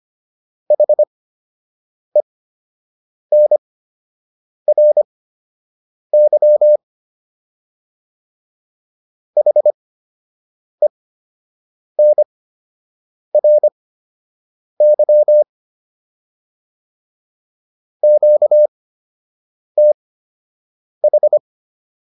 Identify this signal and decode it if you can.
Morse code, heard as a keyed tone, about 25 wpm, spaced as for 7 wpm, 600 Hz. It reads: HENRY HENRY QTH